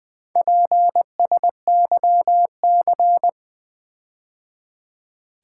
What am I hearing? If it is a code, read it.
Morse code, heard as a keyed tone, about 20 wpm, 700 Hz. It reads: PSYC
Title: Build A Word - 4 Letters At A Time - 12-Letter Words - 20wpm